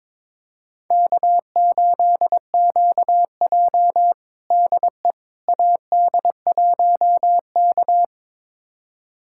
Morse code, heard as a keyed tone, about 22 wpm, 700 Hz. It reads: K8QJ DE AD1K
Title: USA Callsign DE Callsign - 4 Characters - 22wpm